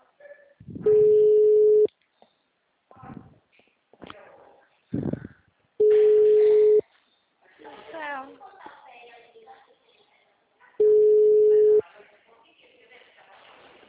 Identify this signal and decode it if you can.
Ringback tone (Europe, 425 Hz), with speech.